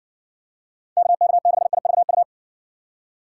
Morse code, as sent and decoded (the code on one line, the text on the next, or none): -.- -.- -.... .. ...- ..-
KK6IVU